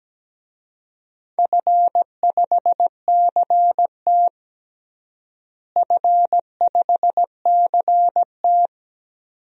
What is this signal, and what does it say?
Morse code, heard as a keyed tone, about 17 wpm, 700 Hz. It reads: F5CT F5CT
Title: DXCC-100 Call Signs - 4 Characters - Repeated 2x - 17wpm